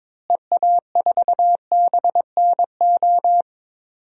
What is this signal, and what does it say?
Morse code, heard as a keyed tone, about 22 wpm, 700 Hz. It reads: EA4BNO
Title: DXCC-100 Call Signs - 6 Characters - Repeated 2x - 22wpm